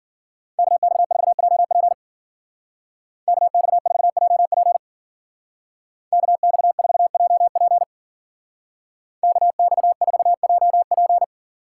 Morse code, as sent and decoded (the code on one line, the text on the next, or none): -.- -..- ....- .--- .--. / -.- -..- ....- .--- .--. / -.- -..- ....- .--- .--. / -.- -..- ....- .--- .--.
KX4JP KX4JP KX4JP KX4JP